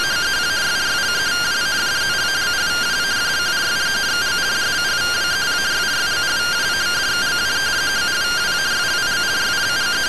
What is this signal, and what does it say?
Two-tone teleprinter signal, about 170 Hz shift.
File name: findings / MR AXIAL 0mm 6mm.gen-04 rtty rx